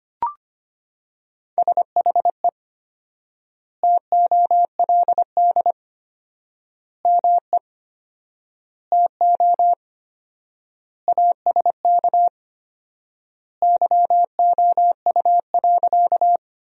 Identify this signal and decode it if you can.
Morse code, heard as a keyed tone, about 25 wpm, 700 Hz. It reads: SHE TOLD ME TO ASK YOU.
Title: 4x Word-Spacing - Sentences from Top 400 Words - 25wpm